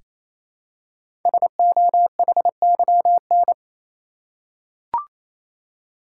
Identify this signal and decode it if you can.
Morse code, heard as a keyed tone, about 28 wpm, 700 Hz. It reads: SOHYN
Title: Code Groups - 5 Letters - 28wpm